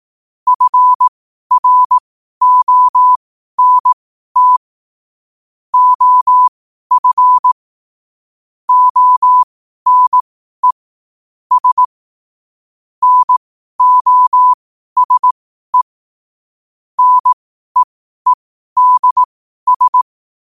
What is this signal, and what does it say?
Morse code, heard as a keyed tone, about 18 wpm, spaced as for 12 wpm, 1000 Hz. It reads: FRONT OF ONE S NOSE NEEDS
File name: qotdfiles / 12wpm